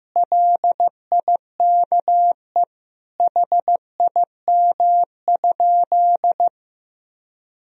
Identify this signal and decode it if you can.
Morse code, heard as a keyed tone, about 15 wpm, 700 Hz. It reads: LIKE HIM?
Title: Sentences from Top 100 Words and Abbreviations 15wpm